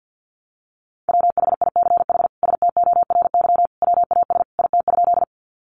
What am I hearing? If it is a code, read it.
Morse code, heard as a keyed tone, about 50 wpm, 700 Hz. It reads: WHICH STORY WAS IT?
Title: Sentences from Top 300 Words 50wpm